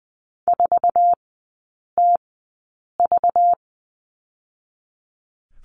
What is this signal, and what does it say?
Morse code, heard as a keyed tone, about 20 wpm, spaced as for 8 wpm, 690 Hz. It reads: 4TV
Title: 212 Lesson 34e - Number 8 - Three Characters 20wpm